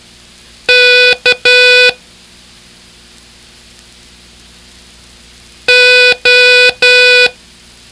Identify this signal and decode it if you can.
Morse code, heard as a keyed tone, about 12 wpm, 2500 Hz. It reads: K O